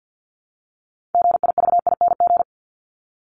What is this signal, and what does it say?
Morse code, heard as a keyed tone, about 50 wpm, 700 Hz. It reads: GIVING
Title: ING Suffix - Top 100 Words with Base Words 50wpm